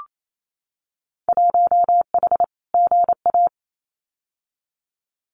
Morse code, heard as a keyed tone, about 28 wpm, 700 Hz. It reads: 1H GA
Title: Winter Field Day Exchanges 28wpm